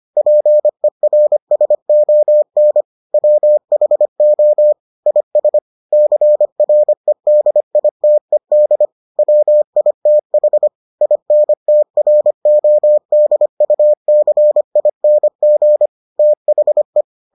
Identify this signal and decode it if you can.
Morse code, heard as a keyed tone, about 25 wpm, 600 Hz. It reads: PERSON WHO IS CREDITED WITH INTRODUCING THE